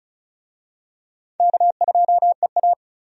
Morse code, heard as a keyed tone, about 36 wpm, 700 Hz. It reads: K2EA
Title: USA Call Signs - 4 Characters - 35wpm